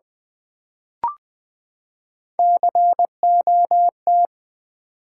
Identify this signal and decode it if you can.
Morse code, heard as a keyed tone, about 20 wpm, 700 Hz. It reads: COT
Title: Build A Word - 3 Letters At A Time - 6-Letter Words - 20wpm